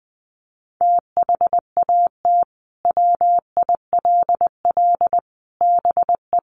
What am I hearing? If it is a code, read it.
Morse code, heard as a keyed tone, about 20 wpm, 700 Hz. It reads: THAT WILL BE
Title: Sentences from Top 1400 Words 20wpm